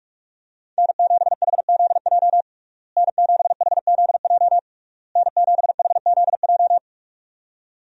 Morse code, heard as a keyed tone, about 46 wpm, 700 Hz. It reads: N7HZJ N7HZJ N7HZJ
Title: USA Call Signs - 5 Characters - Repeated 3x - 45wpm